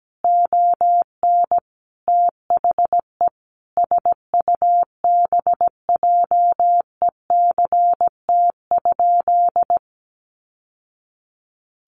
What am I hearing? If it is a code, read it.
Morse code, heard as a keyed tone, about 17 wpm, 700 Hz. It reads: ON THE SUBJECT?
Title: Sentences from Top 800 Words 17wpm